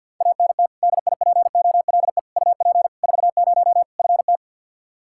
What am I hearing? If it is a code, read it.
Morse code, heard as a keyed tone, about 48 wpm, 700 Hz. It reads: ANT DIPOLE UP 40 FT
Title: Practice QSO - Exchange 2 - Info - 50wpm